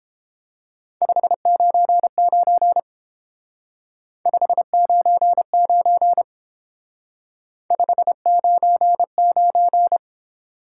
Morse code, heard as a keyed tone, about 29 wpm, 700 Hz. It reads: 599 599 599